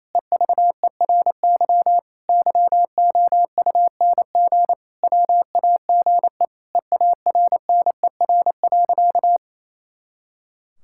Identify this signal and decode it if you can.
Morse code, heard as a keyed tone, about 28 wpm, 700 Hz. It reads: EVERY YOUNG WAGE EARNER.
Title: Pangram Challenge v2 - 28wpm